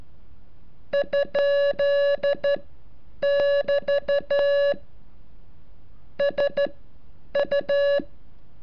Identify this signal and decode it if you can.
Morse code, heard as a keyed tone, about 12 wpm, spaced as for 8 wpm, 570 Hz. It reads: ? <BT> SU
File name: curso / ejercicio 34